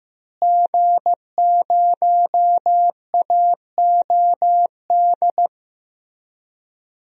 Morse code, heard as a keyed tone, about 15 wpm, 700 Hz. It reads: G0AOD
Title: DXCC-100 Call Signs - 5 Characters - Repeated 2x - 15wpm